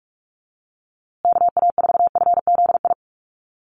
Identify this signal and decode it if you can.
Morse code, heard as a keyed tone, about 44 wpm, 700 Hz. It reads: KA4FZI
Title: USA Call Signs Repeated 2x 45wpm